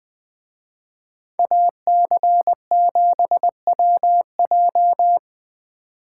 Morse code, heard as a keyed tone, about 20 wpm, 700 Hz. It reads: AC7WJ